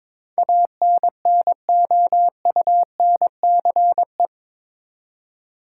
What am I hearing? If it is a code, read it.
Morse code, heard as a keyed tone, about 22 wpm, 700 Hz. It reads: ANNOUNCE
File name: practice / 22wpm / Eight Letter Words - 22wpm